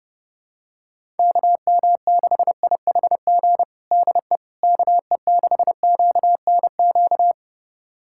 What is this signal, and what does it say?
Morse code, heard as a keyed tone, about 30 wpm, 700 Hz. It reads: KM6IHG DE KE6QNQ
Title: USA Callsign DE Callsign - 6 Characters - 30wpm